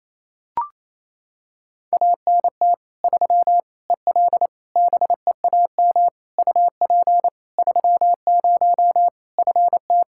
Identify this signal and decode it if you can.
Morse code, heard as a keyed tone, about 28 wpm, 700 Hz. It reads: ANT 3 EL BEAM UP 30 FT
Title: Practice QSO - Exchange 2 - Info - 28wpm